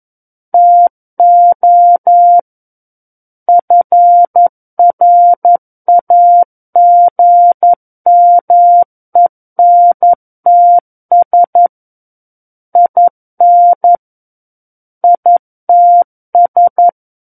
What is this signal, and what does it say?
Morse code, heard as a keyed tone, about 11 wpm, 700 Hz. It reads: TO FRAGMENTS IN ITS
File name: 11 WPM